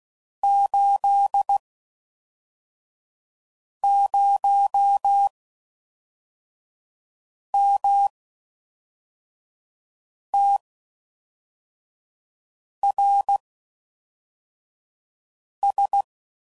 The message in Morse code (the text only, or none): 80MTRS